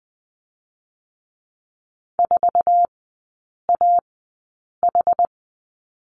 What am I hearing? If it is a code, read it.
Morse code, heard as a keyed tone, about 20 wpm, spaced as for 8 wpm, 690 Hz. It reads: 4AH
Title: 166 Lesson 27e - Number 7 - Three Characters 20wpm